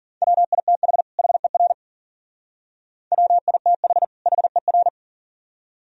Morse code, heard as a keyed tone, about 52 wpm, 700 Hz. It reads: WITH HER WITH HER